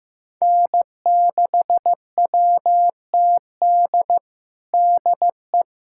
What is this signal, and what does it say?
Morse code, heard as a keyed tone, about 15 wpm, 700 Hz. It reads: N6WTD DE
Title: USA Callsign DE Callsign - 5 Characters - 15wpm